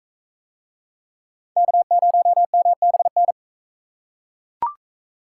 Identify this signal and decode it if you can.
Morse code, heard as a keyed tone, about 39 wpm, 700 Hz. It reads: K0MBN